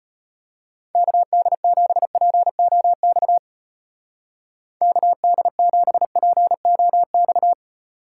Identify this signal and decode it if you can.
Morse code, heard as a keyed tone, about 31 wpm, 700 Hz. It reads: KD7POX KD7POX
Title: Speed Racing - USA 2x3 Call Signs 25wpm